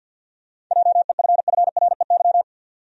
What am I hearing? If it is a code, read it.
Morse code, heard as a keyed tone, about 50 wpm, 700 Hz. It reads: JEFFREY
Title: USA Common Names - 50wpm